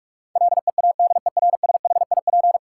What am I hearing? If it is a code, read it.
Morse code, heard as a keyed tone, about 46 wpm, 700 Hz. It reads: LEADERSHIP
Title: Sentences from Top 1600 Words - New Word Drill - 45wpm